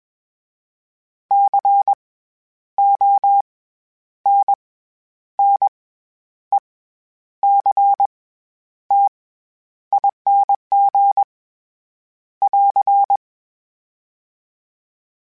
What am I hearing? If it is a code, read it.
Morse code, heard as a keyed tone, about 20 wpm, spaced as for 10 wpm, 800 Hz. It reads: CONNECTING <AR>
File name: Special INGs-mp3s / Special INGs-mp3s / SlowToFaster ing @20